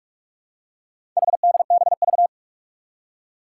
Morse code, heard as a keyed tone, about 46 wpm, 700 Hz. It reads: HDBV
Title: Four Challenging Letter-Number Combinations 45wpm